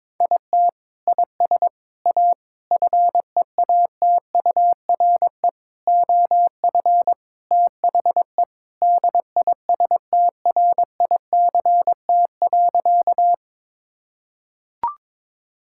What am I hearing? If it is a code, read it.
Morse code, heard as a keyed tone, about 21 wpm, 700 Hz. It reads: IT IS A FEATURE OF THE DISTRICT.
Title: Sentences from Top 1600 Words 22wpm